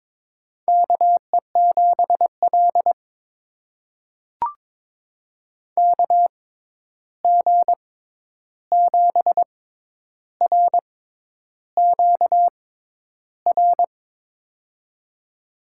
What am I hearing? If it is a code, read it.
Morse code, heard as a keyed tone, about 22 wpm, spaced as for 10 wpm, 700 Hz. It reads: KE7L KG7RQR
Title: USA Call Signs - 6x Character Spacing - 22wpm